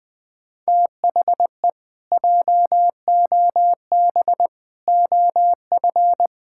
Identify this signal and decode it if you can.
Morse code, heard as a keyed tone, about 19 wpm, 700 Hz. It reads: THE JOB OF
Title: Pangram Challenge v2 - 20wpm